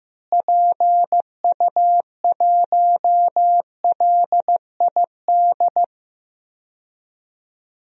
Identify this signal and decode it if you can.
Morse code, heard as a keyed tone, about 14 wpm, 700 Hz. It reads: PU1LID
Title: DXCC-100 Call Signs - 6 Characters - Repeated 3x - 15wpm